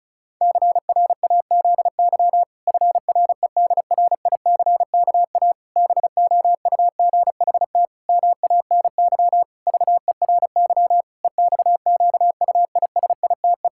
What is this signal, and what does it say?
Morse code, heard as a keyed tone, about 36 wpm, 700 Hz. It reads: CRAZY FREDRICKA BOUGHT MANY VERY EXQUISITE
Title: Pangram Challenge v2 - 35wpm